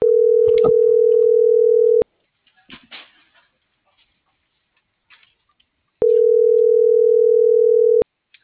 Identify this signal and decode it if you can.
Ringback tone (North America), 440 + 480 Hz.